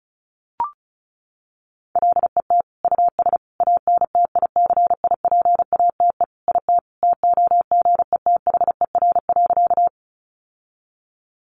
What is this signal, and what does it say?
Morse code, heard as a keyed tone, about 34 wpm, 700 Hz. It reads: LET US ANTICIPATE IT TOGETHER.
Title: Sentences from Top 1500 Words 35wpm